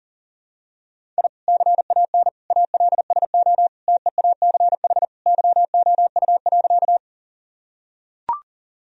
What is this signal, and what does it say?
Morse code, heard as a keyed tone, about 40 wpm, 700 Hz. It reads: I CAN ALSO TEACH YOU.
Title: Sentences from Top 1000 Words - Review Part 2 - 40wpm